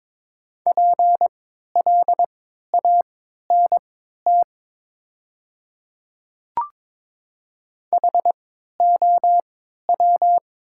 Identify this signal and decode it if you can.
Morse code, heard as a keyed tone, about 22 wpm, spaced as for 12 wpm, 700 Hz. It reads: PLANT HOW
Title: 3x Character Spacing - Top 300 Words - 22wpm